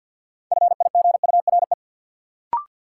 Morse code, heard as a keyed tone, about 47 wpm, 700 Hz. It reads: FIGURE
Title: Sentences from Top 1000 Words - Review Part 2 - New Word Drill - 50wpm